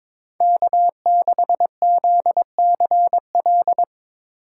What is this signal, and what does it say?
Morse code, heard as a keyed tone, about 22 wpm, 700 Hz. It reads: K6ZCL
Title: K1USN SST QSO Practice - 22wpm